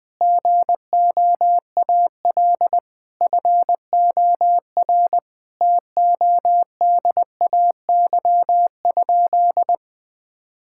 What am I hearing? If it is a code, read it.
Morse code, heard as a keyed tone, about 20 wpm, 700 Hz. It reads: GOAL FOR TODAY?
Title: Sentences from Top 1000 Words - Review Part 5 - 20wpm